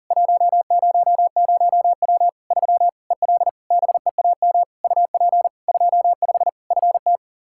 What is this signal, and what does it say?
Morse code, heard as a keyed tone, about 40 wpm, 700 Hz. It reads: 100W 3 EL BEAM UP 25 FT
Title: Practice QSO - Exchange 2 - Info - 40wpm